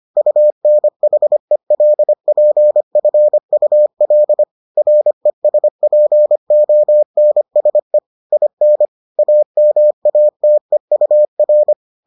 Morse code, heard as a keyed tone, about 25 wpm, 600 Hz. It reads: UNHELPFUL RESPONSE IN AMATEUR